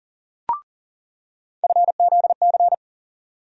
Morse code, heard as a keyed tone, about 37 wpm, 700 Hz. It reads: FZC
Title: USA Call Sign Postfixes - 3 Letters 40wpm